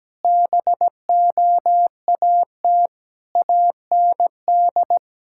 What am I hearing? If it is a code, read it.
Morse code, heard as a keyed tone, about 17 wpm, 700 Hz. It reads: BOAT AND
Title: Sentences from Top 1600 Words 17wpm